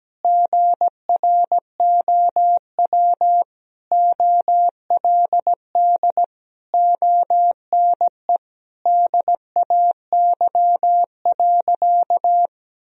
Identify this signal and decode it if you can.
Morse code, heard as a keyed tone, about 17 wpm, 700 Hz. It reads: GROW OLD ONE DAY.
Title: Sentences from Top 1000 Words - Review Part 1 - 17wpm